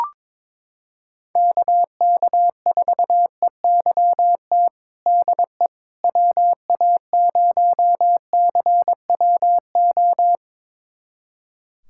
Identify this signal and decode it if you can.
Morse code, heard as a keyed tone, about 22 wpm, 700 Hz. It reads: KK4EYT DE WA0CWO
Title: USA Callsign DE Callsign - 6 Characters - 22wpm